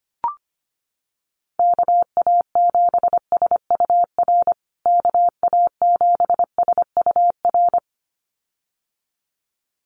Morse code, heard as a keyed tone, about 25 wpm, 700 Hz. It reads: KA7SUR KA7SUR
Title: USA Call Signs Repeated 2x 25wpm